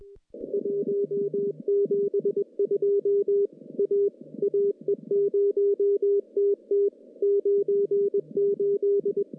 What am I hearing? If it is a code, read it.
Morse code, heard as a keyed tone, about 22 wpm, 400 Hz. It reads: 172 A AE0TT 98
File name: contests / CW-SS-k5tr- -1300